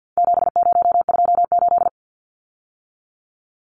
Morse code, heard as a keyed tone, about 50 wpm, 700 Hz. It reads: 7028
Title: Numbers - 4-Digit 50wpm